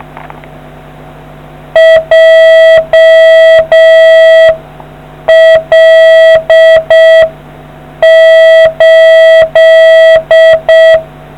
Morse code, heard as a keyed tone, about 6 wpm, 1950 Hz. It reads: JL8